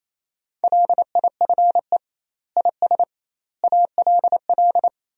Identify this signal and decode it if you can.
Morse code, heard as a keyed tone, about 28 wpm, 700 Hz. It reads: LIFE IS ALL
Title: 2x Word-Spacing - Sentences from Top 300 Words - 28wpm